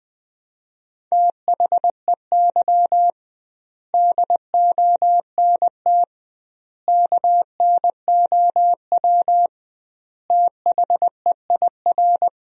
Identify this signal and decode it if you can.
Morse code, heard as a keyed tone, about 20 wpm, 700 Hz. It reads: THEY DONT KNOW THEIR
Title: 2x Word-Spacing - Sentences from Top 300 Words - 20wpm